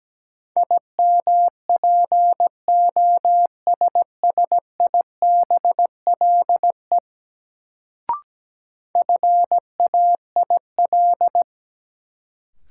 Morse code, heard as a keyed tone, about 17 wpm, 700 Hz. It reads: IMPOSSIBLE FAIL
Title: Sentences from Top 1100 Words - New Word Drill - 17wpm